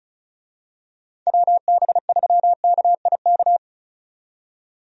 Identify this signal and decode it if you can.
Morse code, heard as a keyed tone, about 36 wpm, 700 Hz. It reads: WB3KIK
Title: USA Call Signs - 6 Characters - 35wpm